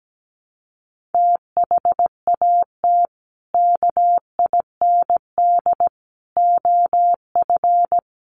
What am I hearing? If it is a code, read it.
Morse code, heard as a keyed tone, about 17 wpm, 700 Hz. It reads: THAT KIND OF
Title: Sets of 3 Words 17wpm